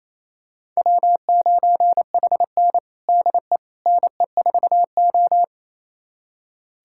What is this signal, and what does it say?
Morse code, heard as a keyed tone, about 28 wpm, 700 Hz. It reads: W9HN DE NE4O
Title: USA Callsign DE Callsign - 4 Characters - 28wpm